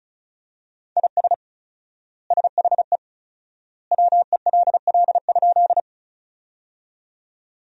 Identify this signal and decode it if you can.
Morse code, heard as a keyed tone, about 34 wpm, 700 Hz. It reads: IS SHE WELL?
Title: 4x Word-Spacing - Sentences from Top 200 Words - 35wpm